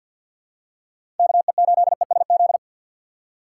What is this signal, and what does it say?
Morse code, heard as a keyed tone, about 52 wpm, 700 Hz. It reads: KE8ESZ
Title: USA Call Signs - 2x Character Spacing - 50wpm